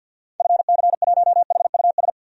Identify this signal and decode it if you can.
Morse code, heard as a keyed tone, about 47 wpm, 700 Hz. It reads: FC1HUS
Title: DXCC-100 Call Signs - 6 Characters - 50wpm